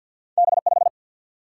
Morse code, heard as a keyed tone, about 53 wpm, 700 Hz. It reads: B 5
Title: Single-Letter Single-Number 50wpm